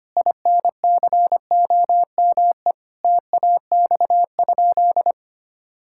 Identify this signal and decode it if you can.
Morse code, heard as a keyed tone, about 25 wpm, 700 Hz. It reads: INCOME TAX?
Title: Sentences from Top 1000 Words - Review Part 5 - 25wpm